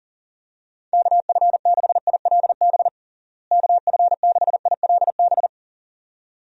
Morse code, heard as a keyed tone, about 40 wpm, 700 Hz. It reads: KF6ILB KF6ILB
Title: USA Call Signs - 6 Characters - Repeated 2x - 40wpm